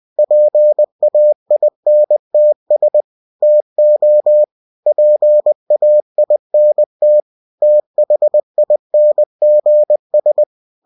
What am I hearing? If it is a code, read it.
Morse code, heard as a keyed tone, about 20 wpm, 600 Hz. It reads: PAINTS TO PAINT THINGS